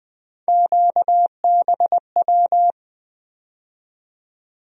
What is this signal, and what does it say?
Morse code, heard as a keyed tone, about 20 wpm, 700 Hz. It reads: QBW
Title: USA Call Sign Postfixes - 3 Letters 20wpm